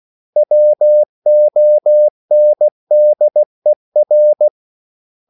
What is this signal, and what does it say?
Morse code, heard as a keyed tone, about 16 wpm, 600 Hz. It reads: WONDER
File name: CW / Practice / 500MostCommonEnglishWords16